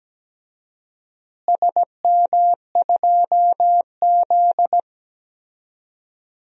Morse code, heard as a keyed tone, about 17 wpm, 700 Hz. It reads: SM2Z
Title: DXCC-100 Call Signs - 4 Characters - Repeated 3x - 17wpm